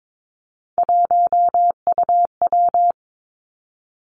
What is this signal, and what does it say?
Morse code, heard as a keyed tone, about 22 wpm, 700 Hz. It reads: 1UW
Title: Three Challenging Letter-Number Combinations 22wpm